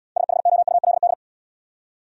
Morse code, heard as a keyed tone, about 76 wpm, 700 Hz. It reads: SHOULD